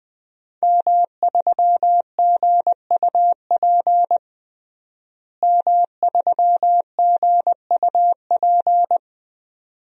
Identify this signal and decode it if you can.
Morse code, heard as a keyed tone, about 20 wpm, 700 Hz. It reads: M3GUP M3GUP
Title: DXCC-100 Call Signs - 5 Characters - Repeated 2x - 20wpm